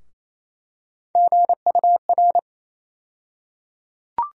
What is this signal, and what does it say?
Morse code, heard as a keyed tone, about 28 wpm, 700 Hz. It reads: GUR
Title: USA Call Sign Postfixes - 3 Letters 28wpm